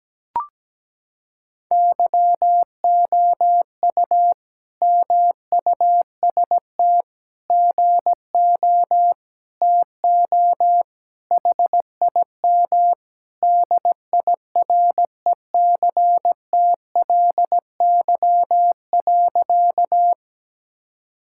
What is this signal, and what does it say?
Morse code, heard as a keyed tone, about 17 wpm, 700 Hz. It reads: YOU MUST GO TO HIM DIRECTLY.
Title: Sentences from Top 1000 Words - Review Part 5 - 17wpm